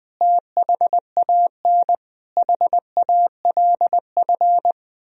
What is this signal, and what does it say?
Morse code, heard as a keyed tone, about 20 wpm, 700 Hz. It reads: THAN HALF
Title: Sentences from Top 1000 Words - Review Part 2 - 20wpm